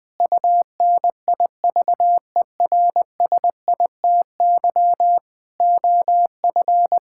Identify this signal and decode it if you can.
Morse code, heard as a keyed tone, about 20 wpm, 700 Hz. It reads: UNIVERSITY OF
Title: Sentences with 4-Word Phrases - 20wpm